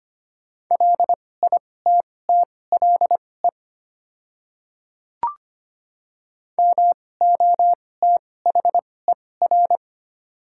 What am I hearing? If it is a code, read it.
Morse code, heard as a keyed tone, about 25 wpm, spaced as for 18 wpm, 700 Hz. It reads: LITTLE MOTHER
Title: 2x Character Spacing - Top 300 Words - 25wpm